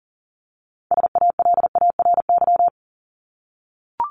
Morse code, heard as a keyed tone, about 40 wpm, 700 Hz. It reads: SALARY
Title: Top 1751-2000 Words 40wpm